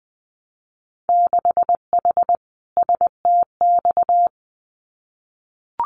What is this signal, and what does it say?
Morse code, heard as a keyed tone, about 19 wpm, 700 Hz. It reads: 6H STX
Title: Winter Field Day Exchanges Encore 20wpm